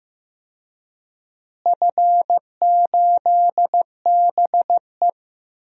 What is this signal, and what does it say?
Morse code, heard as a keyed tone, about 15 wpm, 700 Hz. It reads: F8BE
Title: DXCC-100 Call Signs - 4 Characters - Repeated 2x - 15wpm